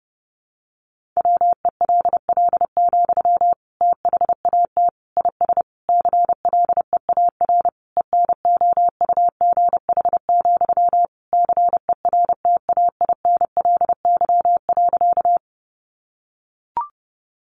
Morse code, heard as a keyed tone, about 30 wpm, 700 Hz. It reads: WELL, THAT IS CLEAR ENOUGH, CERTAINLY.
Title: Sentences from Top 900 Words 30wpm